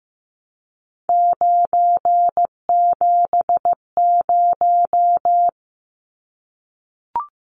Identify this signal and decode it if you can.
Morse code, heard as a keyed tone, about 15 wpm, 700 Hz. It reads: 970
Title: Numbers - 3-Digit 15wpm